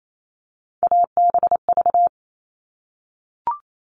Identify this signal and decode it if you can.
Morse code, heard as a keyed tone, about 28 wpm, 700 Hz. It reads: ABV
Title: All Cut Numbers - 3-Digit 28wpm